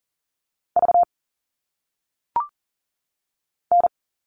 Morse code, heard as a keyed tone, about 39 wpm, 700 Hz. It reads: V N
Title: All Cut Numbers - 1-Digit 40wpm